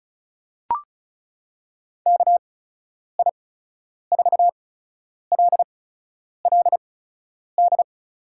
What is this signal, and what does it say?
Morse code, heard as a keyed tone, about 36 wpm, spaced as for 10 wpm, 700 Hz. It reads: KI4LLD
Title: USA Call Signs - 6 Characters - 8x Character Spacing - 35wpm